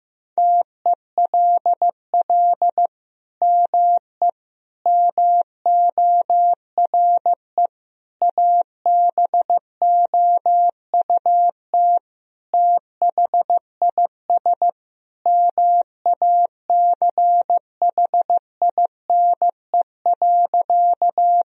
Morse code, heard as a keyed tone, about 16 wpm, 700 Hz. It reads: TELL ME MORE ABOUT THIS MACHINE.
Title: Sentences from Top 1000 Words - Review Part 2 - 15wpm